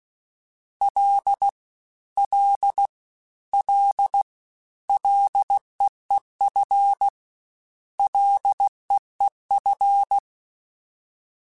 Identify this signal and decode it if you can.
Morse code, heard as a keyed tone, about 16 wpm, 800 Hz. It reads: L L L LEEF LEEF
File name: Short Runs / Short Runs-mp3s / SinglePrefixes @15